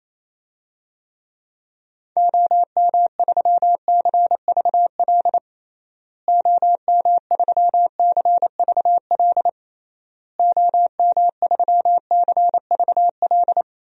Morse code, heard as a keyed tone, about 28 wpm, 700 Hz. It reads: OM3CVL OM3CVL OM3CVL
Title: DXCC-100 Call Signs - 6 Characters - Repeated 3x - 28wpm